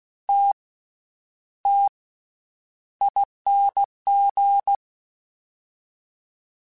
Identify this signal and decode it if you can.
Morse code, heard as a keyed tone, about 16 wpm, spaced as for 9 wpm, 800 Hz. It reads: TTING
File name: Special INGs / Special INGs-mp3s / VerySlowToFaster Ing @15